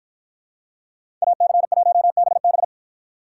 Morse code, heard as a keyed tone, about 50 wpm, 700 Hz. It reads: AC1BB